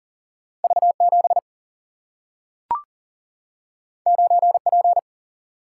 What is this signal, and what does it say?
Morse code, heard as a keyed tone, about 40 wpm, 700 Hz. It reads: V 7 9 P
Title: Single-Letter Single-Number 40wpm